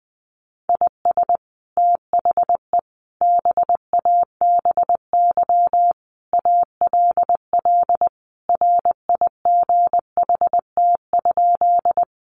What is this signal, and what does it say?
Morse code, heard as a keyed tone, about 20 wpm, 700 Hz. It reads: IS THE BABY ALL RIGHT?
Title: Sentences from Top 1300 Words 20wpm